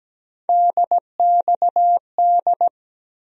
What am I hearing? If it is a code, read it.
Morse code, heard as a keyed tone, about 17 wpm, 700 Hz. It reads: DXD
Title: USA Call Sign Postfixes - 3 Letters 17wpm